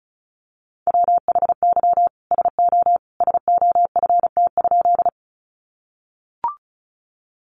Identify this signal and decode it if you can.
Morse code, heard as a keyed tone, about 34 wpm, 700 Hz. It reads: WHY SO SOFT?
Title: Sentences from Top 1500 Words 35wpm